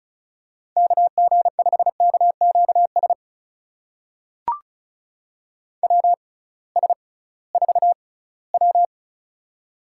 Morse code, heard as a keyed tone, about 36 wpm, spaced as for 20 wpm, 700 Hz. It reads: KG5KQS WS4W
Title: USA Call Signs - 6x Character Spacing - 35wpm